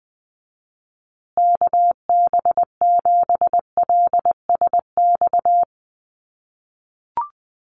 Morse code, heard as a keyed tone, about 20 wpm, 700 Hz. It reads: KB7LSX
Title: USA Call Signs Repeated 2x 20wpm